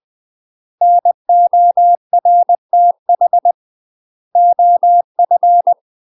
Morse code, heard as a keyed tone, about 20 wpm, 700 Hz. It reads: NORTH OF
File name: UK MASTER QSOs TRAINING / 20WPM / UK MASTER QSOs 0045